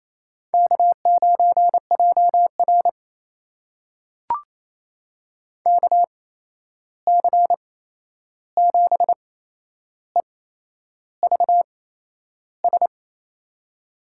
Morse code, heard as a keyed tone, about 28 wpm, spaced as for 11 wpm, 700 Hz. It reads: K9JR KC7EVS
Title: USA Call Signs - 8x Character Spacing - 28wpm